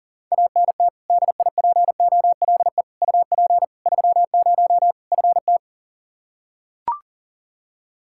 Morse code, heard as a keyed tone, about 40 wpm, 700 Hz. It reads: ANT DIPOLE UP 30 FT
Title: Practice QSO - Exchange 2 - Info - 40wpm